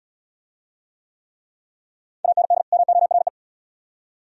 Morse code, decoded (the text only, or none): AND MORE